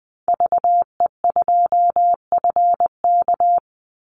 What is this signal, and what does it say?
Morse code, heard as a keyed tone, about 20 wpm, 700 Hz. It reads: VE2FK
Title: Call Signs - 20wpm